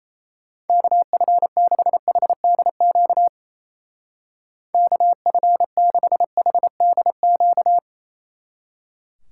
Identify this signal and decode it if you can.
Morse code, heard as a keyed tone, about 36 wpm, 700 Hz. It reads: KF6HDQ KF6HDQ